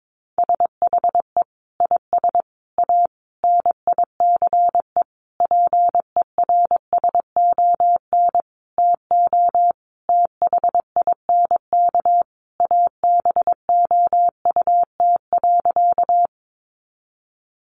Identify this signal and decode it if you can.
Morse code, heard as a keyed tone, about 22 wpm, 700 Hz. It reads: SHE IS A NICE PERSON TO THINK ABOUT.
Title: Sentences from Top 1100 Words 22wpm